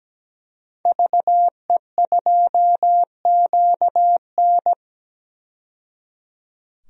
Morse code, heard as a keyed tone, about 17 wpm, 700 Hz. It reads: VE2QN
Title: DXCC-100 Call Signs - 5 Characters - 17wpm